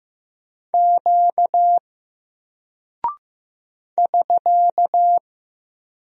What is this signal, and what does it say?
Morse code, heard as a keyed tone, about 15 wpm, 700 Hz. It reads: Q <SK>
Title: Single Character (Letter:Number:Punctuation:Prosign) Alt 15wpm